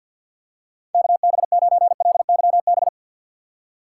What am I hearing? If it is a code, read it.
Morse code, heard as a keyed tone, about 50 wpm, 700 Hz. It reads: KB9LYB
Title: USA Call Signs - 4x Character Spacing - 50wpm